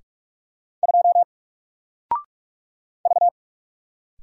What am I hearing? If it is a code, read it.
Morse code, heard as a keyed tone, about 44 wpm, 700 Hz. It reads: 2 V
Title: Single Letter-Number Alt 45wpm